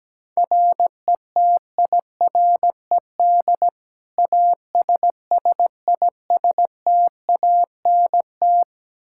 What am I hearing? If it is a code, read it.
Morse code, heard as a keyed tone, about 17 wpm, 700 Hz. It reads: RETIRED ASSISTANT
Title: K1USN SST QSO Practice 17wpm